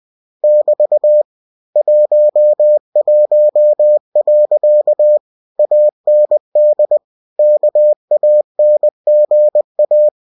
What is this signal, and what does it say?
Morse code, heard as a keyed tone, about 20 wpm, 600 Hz. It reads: <BT> 11. AND KANGA
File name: cw-books / winnie-the-pooh / winnie-20wpm-page-0064